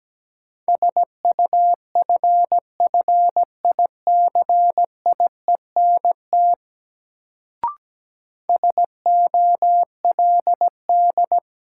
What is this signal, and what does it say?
Morse code, heard as a keyed tone, about 17 wpm, 700 Hz. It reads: SUFFICIENT SOLD
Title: Sentences from Top 1400 Words - New Word Drill Alt - 17wpm